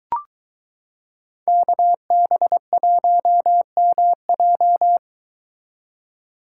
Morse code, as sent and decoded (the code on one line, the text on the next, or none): -.- -... .---- -- .---
KB1MJ